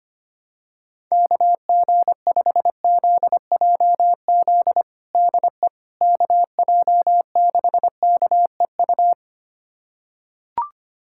Morse code, heard as a keyed tone, about 25 wpm, 700 Hz. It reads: KG5ZJZ DE KJ6KEU
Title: USA Callsign DE Callsign - 6 Characters - 25wpm